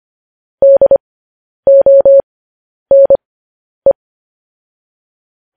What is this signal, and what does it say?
Morse code, heard as a keyed tone, about 25 wpm, spaced as for 10 wpm, 550 Hz. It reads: DONE
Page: news headlines in morse code at 10 wpm